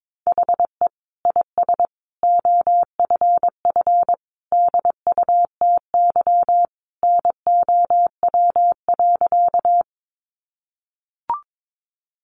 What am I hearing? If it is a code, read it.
Morse code, heard as a keyed tone, about 22 wpm, 700 Hz. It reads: HE IS OFF DUTY NOW.
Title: Sentences from Top 1500 Words 22wpm